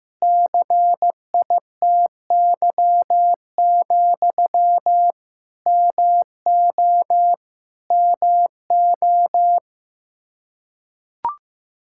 Morse code, heard as a keyed tone, about 15 wpm, 700 Hz. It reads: CITY, MO MO
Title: Practice QSO - Exchange 1 - Beginning - 15wpm